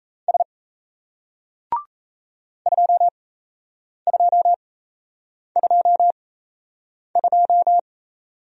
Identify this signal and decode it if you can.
Morse code, heard as a keyed tone, about 36 wpm, spaced as for 8 wpm, 700 Hz. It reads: S 2222